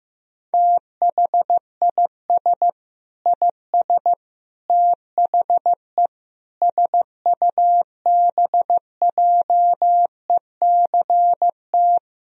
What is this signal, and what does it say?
Morse code, heard as a keyed tone, about 15 wpm, 700 Hz. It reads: THIS IS THE SUBJECT
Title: Sentences from Top 1000 Words - Review Part 4 - 15wpm